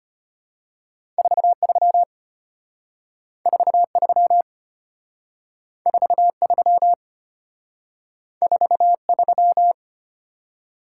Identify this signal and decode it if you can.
Morse code, heard as a keyed tone, about 33 wpm, 700 Hz. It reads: 43 43 43 43